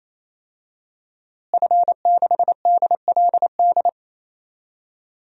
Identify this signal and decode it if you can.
Morse code, heard as a keyed tone, about 27 wpm, 700 Hz. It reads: F6DLD
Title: DXCC-100 Call Signs - 5 Characters - Repeated 3x - 28wpm